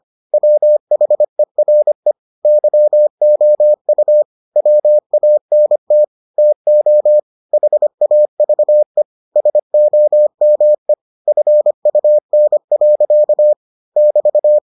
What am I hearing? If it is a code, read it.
Morse code, heard as a keyed tone, about 25 wpm, 600 Hz. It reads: WHERE YOU WANT TO HAVE SOME FUN. <BT>